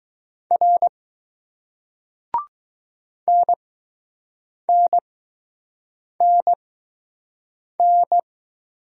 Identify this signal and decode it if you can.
Morse code, heard as a keyed tone, about 20 wpm, spaced as for 6 wpm, 700 Hz. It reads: R NNNN